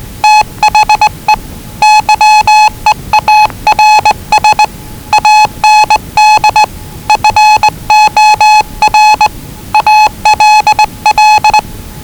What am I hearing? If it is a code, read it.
Morse code, heard as a keyed tone, about 19 wpm, 840 Hz. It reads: THE YEARS AND FOR ALL
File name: WellingtonRadioZlwCloseDownOnHfAndMf / 01-Zlw 4277khz 417.5 qrt until 0925z